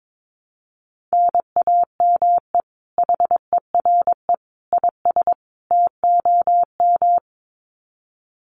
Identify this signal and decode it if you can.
Morse code, heard as a keyed tone, about 22 wpm, 700 Hz. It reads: NAME HERE IS TOM
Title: Common QSO Phrases 22wpm